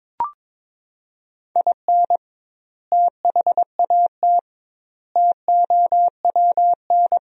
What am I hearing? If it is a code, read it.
Morse code, heard as a keyed tone, about 22 wpm, 700 Hz. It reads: IN THAT TOWN
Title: 2x Word-Spacing - Sentences from Top 500 Words - 22wpm